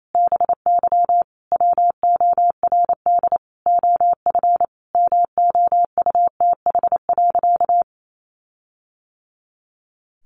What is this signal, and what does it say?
Morse code, heard as a keyed tone, about 28 wpm, 700 Hz. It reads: BY WORD OF MOUTH.
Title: Sentences from Top 1500 Words 28wpm